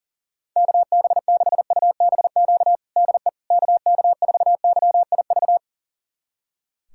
Morse code, heard as a keyed tone, about 40 wpm, 700 Hz. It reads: KB6UBQ DE KK4YIV
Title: USA Callsign DE Callsign - 6 Characters - 40wpm